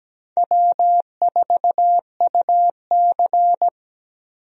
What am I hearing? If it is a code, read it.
Morse code, heard as a keyed tone, about 17 wpm, 700 Hz. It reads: W4UC
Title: USA Call Signs - 4 Characters - 17wpm